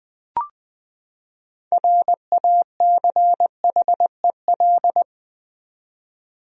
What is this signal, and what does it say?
Morse code, heard as a keyed tone, about 20 wpm, 700 Hz. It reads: RACHEL
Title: USA Common Names - 20wpm